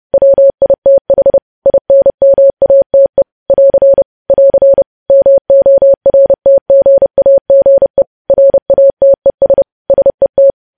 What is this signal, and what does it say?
Morse code, heard as a keyed tone, about 30 wpm, 550 Hz. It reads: WITH INMATE <AR> <AR> MORTGAGE RATES SET
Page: news headlines in morse code at 30 wpm